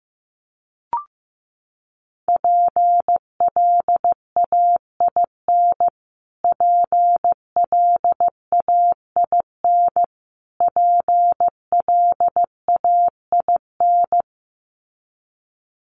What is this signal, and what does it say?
Morse code, heard as a keyed tone, about 15 wpm, 700 Hz. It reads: PLAIN PLAIN PLAIN
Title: Top 500 Words Repeated 15wpm